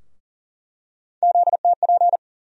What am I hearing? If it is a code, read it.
Morse code, heard as a keyed tone, about 40 wpm, 700 Hz. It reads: ZTP
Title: USA Call Sign Postfixes - 3 Letters 40wpm